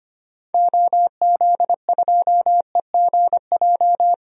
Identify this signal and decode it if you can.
Morse code, heard as a keyed tone, about 25 wpm, 700 Hz. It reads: OZ2EGJ
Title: DXCC-100 Call Signs - 6 Characters - Repeated 2x - 25wpm